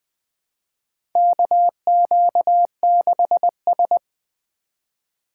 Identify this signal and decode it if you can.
Morse code, heard as a keyed tone, about 20 wpm, 700 Hz. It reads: KQ6S
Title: USA Call Signs - 4 Characters - 4x Character Spacing - 20wpm